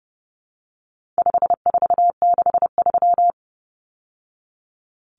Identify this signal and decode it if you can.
Morse code, heard as a keyed tone, about 30 wpm, 700 Hz. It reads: 5463
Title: Numbers - 4-Digit 30wpm